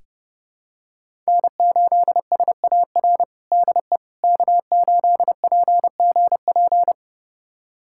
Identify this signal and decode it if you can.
Morse code, heard as a keyed tone, about 30 wpm, 700 Hz. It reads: N8SAR DE K8PGP
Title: USA Callsign DE Callsign - 5 Characters - 30wpm